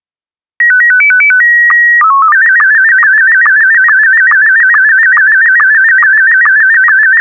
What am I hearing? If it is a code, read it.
SSTV (slow-scan television), Scottie 1.